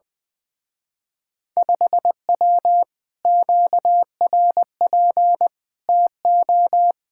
Morse code, heard as a keyed tone, about 20 wpm, 700 Hz. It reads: 5W QRP TO
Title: Speed-Racing Common QSO Phrases - 20wpm